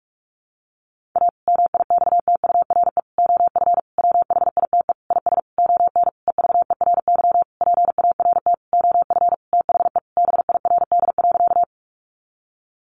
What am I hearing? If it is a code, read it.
Morse code, heard as a keyed tone, about 44 wpm, 700 Hz. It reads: A MIXTURE OF WHITE IS ON EVERY PART OF THE BIRD.
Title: Sentences from Top 1500 Words 45wpm